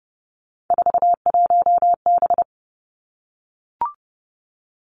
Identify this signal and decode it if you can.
Morse code, heard as a keyed tone, about 30 wpm, 700 Hz. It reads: 41B
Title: Three Challenging Letter-Number Combinations 30wpm